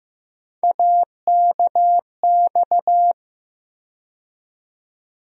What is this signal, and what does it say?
Morse code, heard as a keyed tone, about 15 wpm, 700 Hz. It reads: AKX
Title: USA Call Sign Postfixes - 3 Letters 15wpm